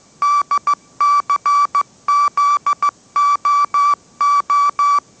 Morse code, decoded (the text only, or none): DCZOO